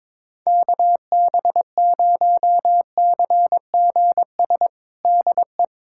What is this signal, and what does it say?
Morse code, heard as a keyed tone, about 22 wpm, 700 Hz. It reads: KB0CGS DE
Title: USA Callsign DE Callsign - 6 Characters - 22wpm